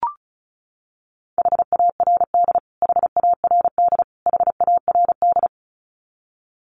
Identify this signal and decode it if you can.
Morse code, heard as a keyed tone, about 34 wpm, 700 Hz. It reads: HARD HARD HARD
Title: Top 300 Words Repeated 35wpm